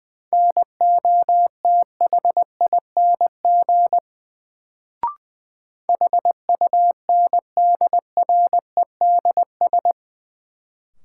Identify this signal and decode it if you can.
Morse code, heard as a keyed tone, about 21 wpm, 700 Hz. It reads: NOTHING HUNDREDS
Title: Sets of 4 Words - New Word Drill - 20wpm